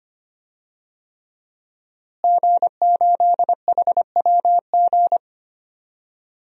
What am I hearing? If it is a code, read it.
Morse code, heard as a keyed tone, about 25 wpm, 700 Hz. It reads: G8HWG
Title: DXCC-100 Call Signs - 5 Characters - Repeated 3x - 25wpm